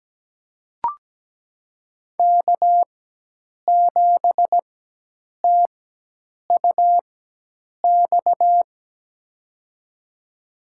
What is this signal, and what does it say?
Morse code, heard as a keyed tone, about 17 wpm, spaced as for 8 wpm, 700 Hz. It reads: K7TUX
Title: USA Call Signs - 5 Characters - 4x Character Spacing - 17wpm